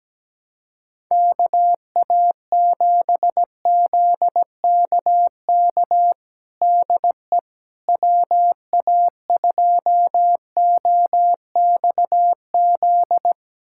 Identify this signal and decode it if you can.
Morse code, heard as a keyed tone, about 17 wpm, 700 Hz. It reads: KA7ZKK DE WA2OXZ
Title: USA Callsign DE Callsign - 6 Characters - 17wpm